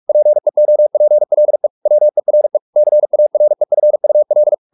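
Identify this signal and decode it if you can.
Morse code, heard as a keyed tone, about 44 wpm, 600 Hz. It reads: PEOPLE WERE CAREFUL
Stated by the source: Morse practice at 45 wpm, the novel Never by Ken Follett in English, from CW med Gnister.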